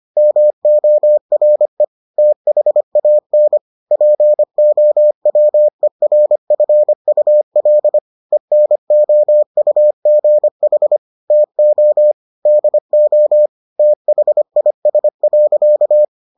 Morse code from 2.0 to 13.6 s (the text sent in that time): THAN POWERFUL ENOUGH TO DO